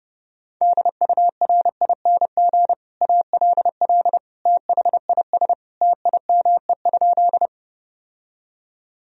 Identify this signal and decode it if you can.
Morse code, heard as a keyed tone, about 30 wpm, 700 Hz. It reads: DURING ALL THIS TIME?
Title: Sentences from Top 1000 Words - Review Part 2 - 30wpm